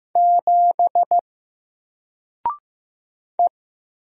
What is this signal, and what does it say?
Morse code, heard as a keyed tone, about 15 wpm, 700 Hz. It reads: 7 E
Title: Single Character (Letter:Number:Punctuation:Prosign) 15wpm